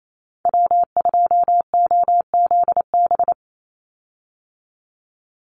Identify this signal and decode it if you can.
Morse code, heard as a keyed tone, about 28 wpm, 700 Hz. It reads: W2OZB
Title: Call Signs Repeated 3x - 28wpm